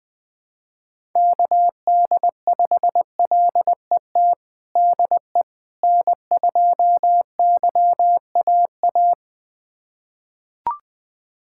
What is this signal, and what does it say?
Morse code, heard as a keyed tone, about 20 wpm, 700 Hz. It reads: KD5LET DE N2YAA
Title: USA Callsign DE Callsign 20wpm